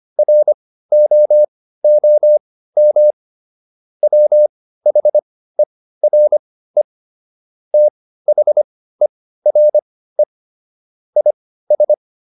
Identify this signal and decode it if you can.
Morse code, heard as a keyed tone, about 25 wpm, spaced as for 15 wpm, 600 Hz. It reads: ROOM WHERE THERE IS